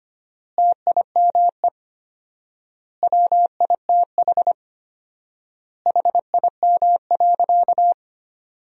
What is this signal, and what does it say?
Morse code, heard as a keyed tone, about 25 wpm, 700 Hz. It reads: TIME WITH HIM.
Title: 4x Word-Spacing - Sentences from Top 300 Words - 25wpm